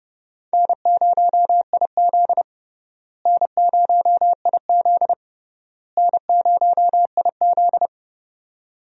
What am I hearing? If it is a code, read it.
Morse code, heard as a keyed tone, about 30 wpm, 700 Hz. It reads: N0IZ N0IZ N0IZ
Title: USA Call Signs - 4 Characters - Repeated 3x - 30wpm